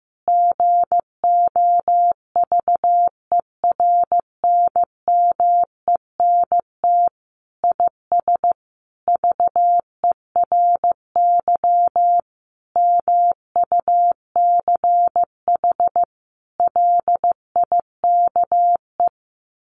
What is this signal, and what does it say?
Morse code, heard as a keyed tone, about 15 wpm, 700 Hz. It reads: GOVERNMENT IS VERY MUCH LIKE
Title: Sentences from Top 1300 Words 15wpm